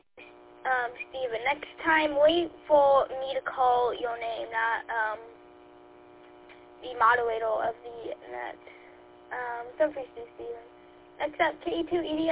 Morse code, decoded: none

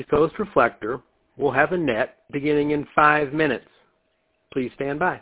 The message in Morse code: none

